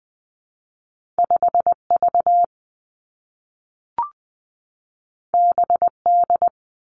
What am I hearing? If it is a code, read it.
Morse code, heard as a keyed tone, about 20 wpm, 700 Hz. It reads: 5 V B D
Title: Two Challenging Letter-Number Combinations 20wpm